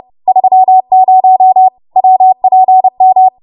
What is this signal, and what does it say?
Morse code, heard as a keyed tone, about 30 wpm, 750 Hz. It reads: 30 WPM
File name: Morse / 30WPM